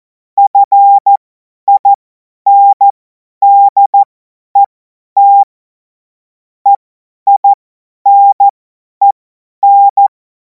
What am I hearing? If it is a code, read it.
Morse code, heard as a keyed tone, about 14 wpm, spaced as for 10 wpm, 800 Hz. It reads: FINDET EINEN
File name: Morsekurs / Witze / Witz 10. Februar 2026 50 F70